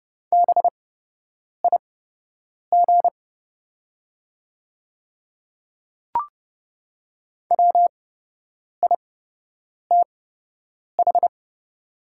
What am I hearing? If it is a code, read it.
Morse code, heard as a keyed tone, about 30 wpm, spaced as for 8 wpm, 700 Hz. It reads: BIG WITH
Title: 8x Character Spacing - Top 200 Words - 30wpm